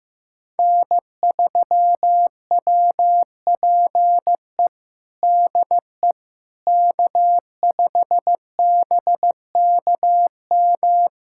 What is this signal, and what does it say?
Morse code, heard as a keyed tone, about 15 wpm, 700 Hz. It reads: N3WPE DE K5BKM
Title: USA Callsign DE Callsign - 5 Characters - 15wpm